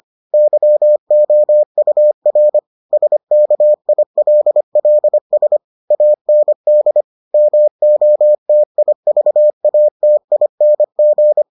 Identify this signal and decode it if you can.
Morse code, heard as a keyed tone, about 25 wpm, 600 Hz. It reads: YOUR SKILLS AND MOTIVATING